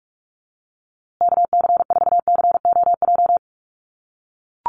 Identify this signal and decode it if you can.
Morse code, heard as a keyed tone, about 44 wpm, 700 Hz. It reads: KC4COJ
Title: USA Call Signs Repeated 2x 45wpm